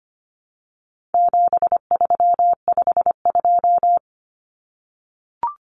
Morse code, heard as a keyed tone, about 25 wpm, 700 Hz. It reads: 7352
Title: Numbers - 4-Digit 25wpm